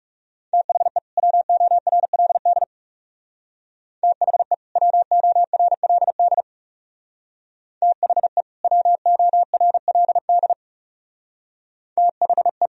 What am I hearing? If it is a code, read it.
Morse code, heard as a keyed tone, about 36 wpm, 700 Hz. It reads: THE WORLD THE WORLD THE WORLD THE
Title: Speed-Racing - Sets of 2 Words 30wpm